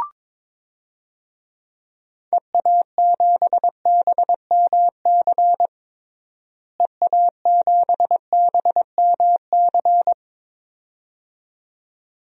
Morse code, heard as a keyed tone, about 22 wpm, 700 Hz. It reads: EA7BMC EA7BMC
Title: DXCC-100 Call Signs - 6 Characters - Repeated 2x - 22wpm